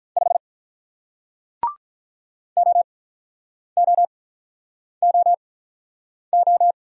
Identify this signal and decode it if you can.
Morse code, heard as a keyed tone, about 45 wpm, spaced as for 9 wpm, 700 Hz. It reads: 5 OOOO